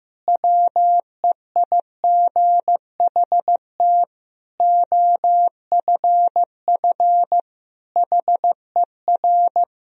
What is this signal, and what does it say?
Morse code, heard as a keyed tone, about 16 wpm, 700 Hz. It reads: WEIGHT OFF HER
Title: Sentences from Top 1000 Words - Review Part 3 - 15wpm